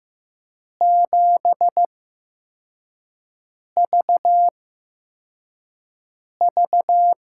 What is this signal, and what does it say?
Morse code, heard as a keyed tone, about 15 wpm, spaced as for 4 wpm, 700 Hz. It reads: 7VV
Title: USA Call Signs - 5 Characters - 8x Character Spacing - 15wpm